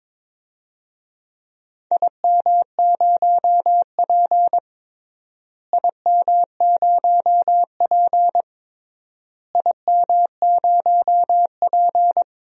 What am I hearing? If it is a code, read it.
Morse code, heard as a keyed tone, about 21 wpm, 700 Hz. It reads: IM0P IM0P IM0P
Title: DXCC-100 Call Signs - 4 Characters - Repeated 3x - 22wpm